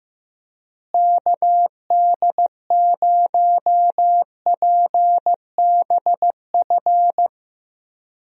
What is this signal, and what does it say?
Morse code, heard as a keyed tone, about 15 wpm, 700 Hz. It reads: KD0PBF
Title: USA Call Signs - 6 Characters - Repeated 3x - 15wpm